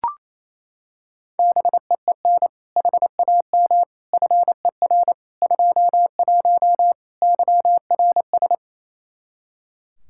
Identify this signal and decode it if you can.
Morse code, heard as a keyed tone, about 28 wpm, 700 Hz. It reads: BEEN HAM FER 21 YRS
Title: Practice QSO - Exchange 3 - More Info - 28wpm